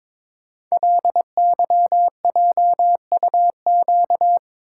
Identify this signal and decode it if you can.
Morse code, heard as a keyed tone, about 22 wpm, 690 Hz. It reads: LYJUQ
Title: Code Groups - 5 Letters - 22wpm